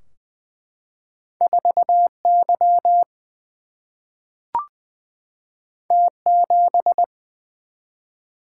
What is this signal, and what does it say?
Morse code, heard as a keyed tone, about 20 wpm, 700 Hz. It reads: 4 Y T 7
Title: Single-Letter Single-Number 20wpm